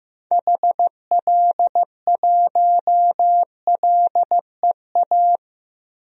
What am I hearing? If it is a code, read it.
Morse code, heard as a keyed tone, about 15 wpm, 700 Hz. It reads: HL1LEA
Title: DXCC-100 Call Signs - 6 Characters - 15wpm